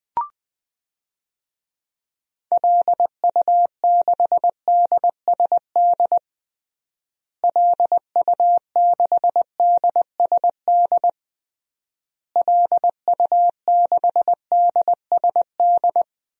Morse code, heard as a keyed tone, about 20 wpm, 700 Hz. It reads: LU6DSD LU6DSD LU6DSD
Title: DXCC-100 Call Signs - 6 Characters - Repeated 3x - 20wpm